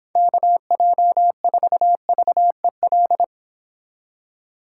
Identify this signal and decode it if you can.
Morse code, heard as a keyed tone, about 25 wpm, 700 Hz. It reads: KJ4VEL